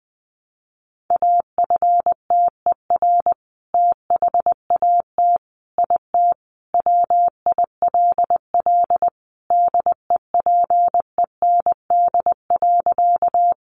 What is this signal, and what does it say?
Morse code, heard as a keyed tone, about 20 wpm, 700 Hz. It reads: AFTER THAT IT WILL DEPEND.
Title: Sentences from Top 1300 Words 20wpm